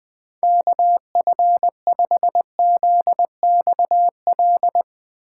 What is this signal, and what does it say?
Morse code, heard as a keyed tone, about 20 wpm, 700 Hz. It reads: KF5ZXL
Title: USA Call Signs - 6 Characters - 3x Character Spacing - 20wpm